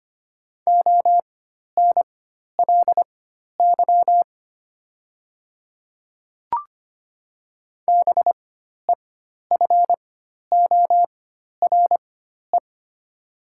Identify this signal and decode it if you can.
Morse code, heard as a keyed tone, about 25 wpm, spaced as for 12 wpm, 700 Hz. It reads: ONLY BEFORE
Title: 4x Character Spacing - Top 200 Words - 25wpm